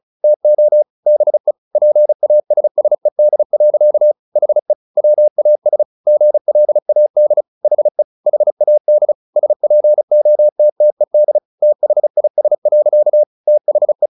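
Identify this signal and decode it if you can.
Morse code, heard as a keyed tone, about 34 wpm, 600 Hz. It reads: TO BE PASSED. HE WAS GLAD HE HAD SPOTTED THIS. THE
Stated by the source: Morse practice at 35 wpm from CW med Gnister, sending Ken Follett's novel Never in English.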